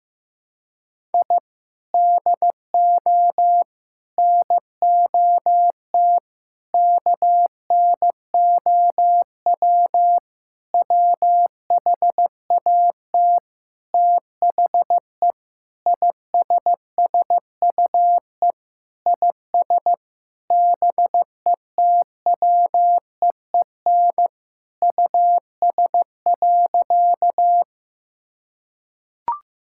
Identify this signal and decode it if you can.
Morse code, heard as a keyed tone, about 15 wpm, 700 Hz. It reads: I DO NOT KNOW WHAT THE ISSUE IS BETWEEN US.
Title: Sentences from Top 1000 Words - Review Part 6 - 15wpm